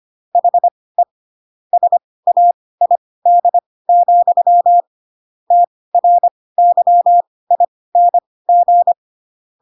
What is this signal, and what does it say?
Morse code, heard as a keyed tone, about 25 wpm, spaced as for 18 wpm, 700 Hz. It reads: HE SAID, TRYING
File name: Pinocchio / Pinocchio ch1 12 25 ik1hge